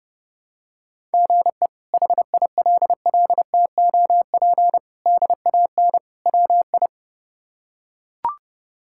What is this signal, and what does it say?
Morse code, heard as a keyed tone, about 32 wpm, 700 Hz. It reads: GE HILLTOP DAN WI
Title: K1USN SST QSO Practice - 30wpm